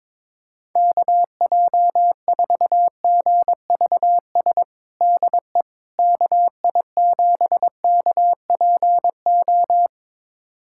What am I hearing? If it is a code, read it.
Morse code, heard as a keyed tone, about 22 wpm, 700 Hz. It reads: KJ4GVS DE KI7KPO
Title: USA Callsign DE Callsign - 6 Characters - 22wpm